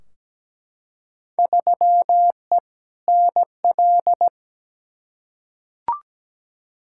Morse code, heard as a keyed tone, about 17 wpm, 700 Hz. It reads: 3E NL
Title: Field Day Exchanges Encore - 17wpm